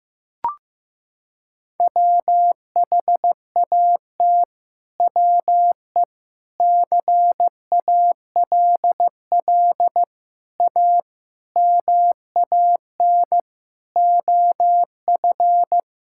Morse code, heard as a keyed tone, about 15 wpm, 700 Hz. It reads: WHAT WE CALL A MAN OF
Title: Sentences from Top 1600 Words 15wpm